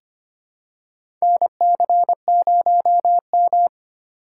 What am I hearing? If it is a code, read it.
Morse code, heard as a keyed tone, about 25 wpm, 700 Hz. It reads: NC0M
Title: USA Call Signs - 4 Characters - 4x Character Spacing - 25wpm